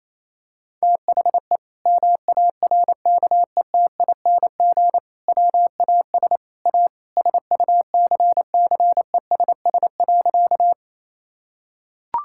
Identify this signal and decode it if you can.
Morse code, heard as a keyed tone, about 28 wpm, 700 Hz. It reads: THE MARKETING WAS A SUCCESS.
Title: Sentences from Top 1000 Words - Review Part 5 - 28wpm